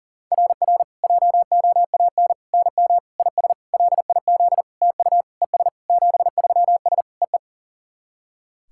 Morse code, heard as a keyed tone, about 44 wpm, 700 Hz. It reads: RR JOAN NM IS LIZ TU ES 73S EE
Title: SOTA QSO Practice 40wpm